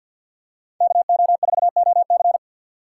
Morse code, heard as a keyed tone, about 51 wpm, 700 Hz. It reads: KO4OC
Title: USA Call Signs - 3x Character Spacing - 50wpm